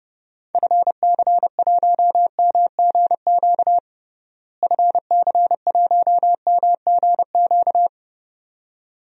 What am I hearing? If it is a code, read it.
Morse code, heard as a keyed tone, about 31 wpm, 700 Hz. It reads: FC1MGQ FC1MGQ
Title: DXCC-100 Call Signs - 6 Characters - Repeated 2x - 30wpm